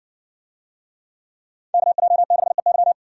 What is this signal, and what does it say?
Morse code, heard as a keyed tone, about 62 wpm, 700 Hz. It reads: KJ6EY